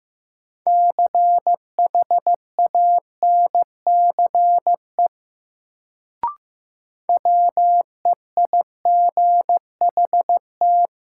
Morse code, heard as a keyed tone, about 15 wpm, 700 Hz. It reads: CHANCE WEIGHT
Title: Sentences from Top 800 Words - New Word Drill - 15wpm